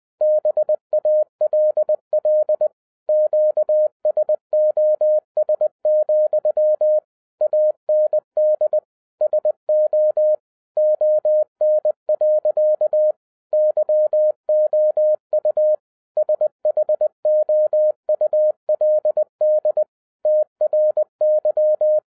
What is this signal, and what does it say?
Morse code, heard as a keyed tone, about 20 wpm, 610 Hz. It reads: BALL QSOS, AND SO ON. YOU SHOULD TRY
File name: 2011-10-mp3 / n13-TIPS FROM MY 34 YEARS OF EXPERIENCE 20wpm